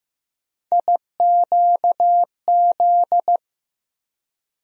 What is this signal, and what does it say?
Morse code, heard as a keyed tone, about 15 wpm, 700 Hz. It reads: IQZ
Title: USA Call Sign Postfixes - 3 Letters 15wpm